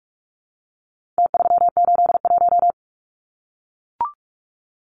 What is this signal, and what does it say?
Morse code, heard as a keyed tone, about 44 wpm, 700 Hz. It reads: T381
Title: Numbers - Contest Style - 4-Digit 45wpm